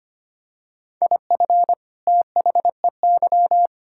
Morse code, heard as a keyed tone, about 25 wpm, 700 Hz. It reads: IF THEY
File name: practice / 25wpm / Sentences with 2-Word Phrases - 25wpm